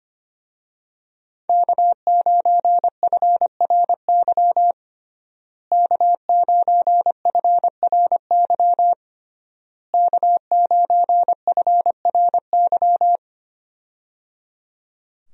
Morse code, heard as a keyed tone, about 25 wpm, 700 Hz. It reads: K9FRY K9FRY K9FRY
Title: USA Call Signs - 5 Characters - Repeated 3x - 25wpm